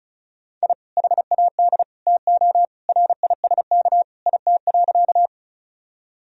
Morse code, heard as a keyed tone, about 36 wpm, 700 Hz. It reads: I HAD TO RISK IT.
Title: Sentences from Top 1000 Words - Review Part 4 - 35wpm